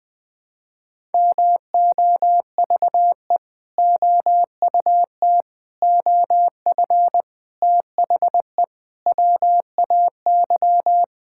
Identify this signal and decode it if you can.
Morse code, heard as a keyed tone, about 20 wpm, 700 Hz. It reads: MOVE OUT OF THE WAY
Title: Sentences with 4-Word Phrases - Spoken Beginning - 20wpm